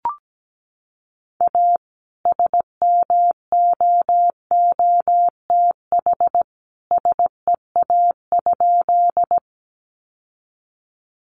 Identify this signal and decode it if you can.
Morse code, heard as a keyed tone, about 17 wpm, 700 Hz. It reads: A SMOOTH SEA?
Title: Sentences from Top 1400 Words 17wpm